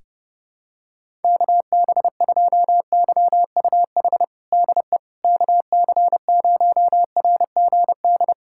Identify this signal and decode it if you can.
Morse code, heard as a keyed tone, about 30 wpm, 700 Hz. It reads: KB2YUH DE KC0RGD
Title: USA Callsign DE Callsign - 6 Characters - 30wpm